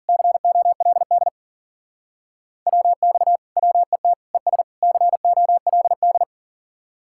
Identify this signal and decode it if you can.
Morse code, heard as a keyed tone, about 51 wpm, 700 Hz. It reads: COLD WX WET ES COLD